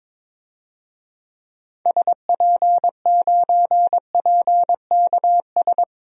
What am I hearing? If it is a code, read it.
Morse code, heard as a keyed tone, about 22 wpm, 700 Hz. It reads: SP9PKS
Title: DXCC-100 Call Signs - 6 Characters - Repeated 2x - 22wpm